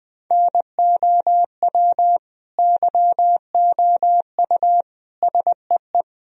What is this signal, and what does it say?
Morse code, heard as a keyed tone, about 20 wpm, 700 Hz. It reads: NOW YOU SEE
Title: Sentences from Top 1000 Words - Review Part 2 - 20wpm